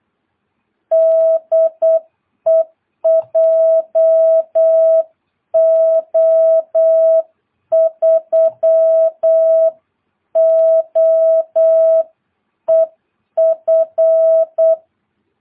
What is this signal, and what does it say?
Morse code, heard as a keyed tone, about 8 wpm, 660 Hz. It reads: DE JO3OEF
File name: jh8cbh / 2022cw